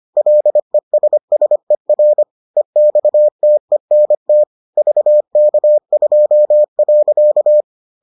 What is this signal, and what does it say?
Morse code, heard as a keyed tone, about 25 wpm, 600 Hz. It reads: LESSER EXTENT VK2.